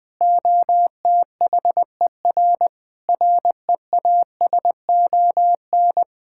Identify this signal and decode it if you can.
Morse code, heard as a keyed tone, about 20 wpm, 700 Hz. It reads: OTHER REASON
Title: Sentences with 7-Word Phrases - 20wpm